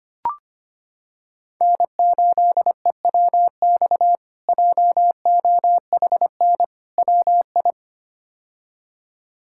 Morse code, heard as a keyed tone, about 25 wpm, 700 Hz. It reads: N8EWX JOHN WI
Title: K1USN SST QSO Practice - 25wpm